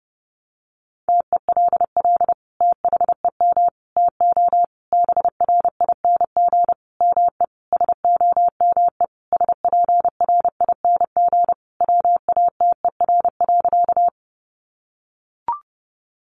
Morse code, heard as a keyed tone, about 30 wpm, 700 Hz. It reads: TELL THEM TO BRING ME SOME SPRING WATER.
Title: Sentences from Top 1200 Words 30wpm